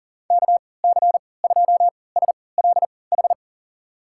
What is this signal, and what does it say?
Morse code, heard as a keyed tone, about 39 wpm, spaced as for 22 wpm, 700 Hz. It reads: KC2SLH
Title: USA Call Signs - 6 Characters - 3x Character Spacing - 40wpm